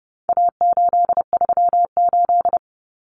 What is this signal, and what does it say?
Morse code, heard as a keyed tone, about 29 wpm, 700 Hz. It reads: A838
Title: Common Cut Numbers - 4-Digit 30wpm